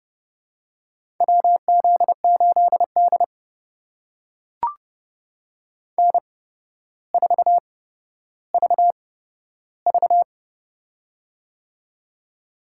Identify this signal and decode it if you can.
Morse code, heard as a keyed tone, about 30 wpm, spaced as for 13 wpm, 700 Hz. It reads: WZ8D N4VV
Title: USA Call Signs - 4 Characters - 8x Character Spacing - 30wpm